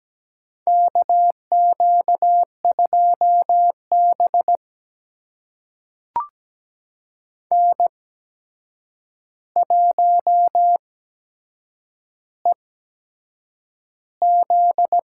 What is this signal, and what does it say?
Morse code, heard as a keyed tone, about 17 wpm, spaced as for 7 wpm, 700 Hz. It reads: KQ2B N1EZ
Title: USA Call Signs - 4 Characters - 8x Character Spacing - 17wpm